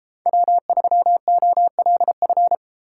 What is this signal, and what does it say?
Morse code, heard as a keyed tone, about 31 wpm, 700 Hz. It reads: W3OLF